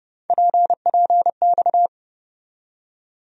Morse code, heard as a keyed tone, about 30 wpm, 700 Hz. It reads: PPX
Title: USA Call Sign Postfixes - 3 Letters 30wpm